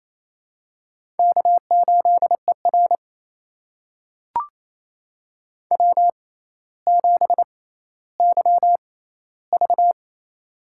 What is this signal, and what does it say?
Morse code, heard as a keyed tone, about 28 wpm, spaced as for 14 wpm, 700 Hz. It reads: K8ER W7YV
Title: USA Call Signs - 4 Characters - 6x Character Spacing - 28wpm